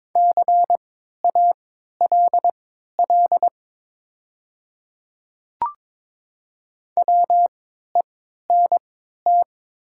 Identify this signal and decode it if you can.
Morse code, heard as a keyed tone, about 22 wpm, spaced as for 12 wpm, 700 Hz. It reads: CALL WENT
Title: 3x Character Spacing - Top 200 Words - 22wpm